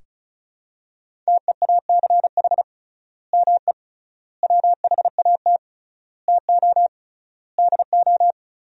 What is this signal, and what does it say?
Morse code, heard as a keyed tone, about 36 wpm, 700 Hz. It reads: TEACH ME WHAT TO DO
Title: 3x Word-Spacing - Sentences from Top 500 Words - 35wpm